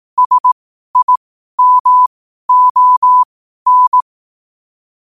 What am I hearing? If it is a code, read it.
Morse code, heard as a keyed tone, about 18 wpm, spaced as for 12 wpm, 1000 Hz. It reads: SIMON